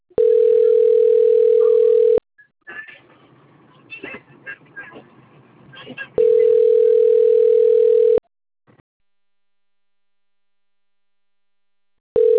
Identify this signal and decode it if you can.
Ringback tone (North America), 440 + 480 Hz.